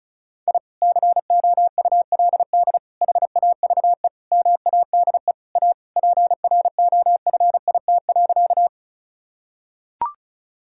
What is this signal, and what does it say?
Morse code, heard as a keyed tone, about 34 wpm, 700 Hz. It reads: I COULD HAVE MADE A PROFIT.
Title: Sentences from Top 1000 Words - Review Part 5 - 35wpm